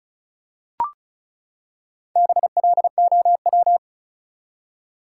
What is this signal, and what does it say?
Morse code, heard as a keyed tone, about 36 wpm, 700 Hz. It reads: BLOW
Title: Four Letter Words - 35wpm